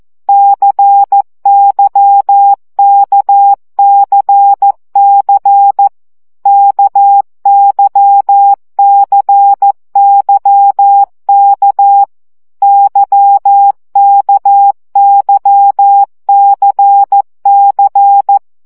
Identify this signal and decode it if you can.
Morse code, heard as a keyed tone, about 14 wpm, 800 Hz. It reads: CYKCC KYCYK YKYCC
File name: CW gyakorlas k c y 12 baud